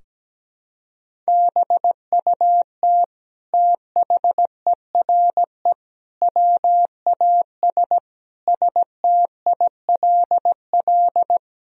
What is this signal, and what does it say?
Morse code, heard as a keyed tone, about 17 wpm, 700 Hz. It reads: BUT THERE WAS STILL
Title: Sentences from Top 1000 Words - Review Part 2 - 17wpm